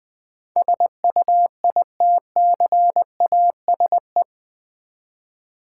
Morse code, heard as a keyed tone, about 20 wpm, 700 Hz. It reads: SUITCASE
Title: Easy Words with Hints - 20wpm